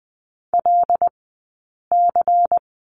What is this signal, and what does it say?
Morse code, heard as a keyed tone, about 20 wpm, 700 Hz. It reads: L C
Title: 120 Lesson 20d - Number 3 - Two Characters 20wpm